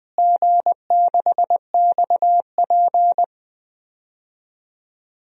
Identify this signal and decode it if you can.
Morse code, heard as a keyed tone, about 20 wpm, 700 Hz. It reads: G6XP
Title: DXCC-100 Call Signs - 4 Characters - Repeated 3x - 20wpm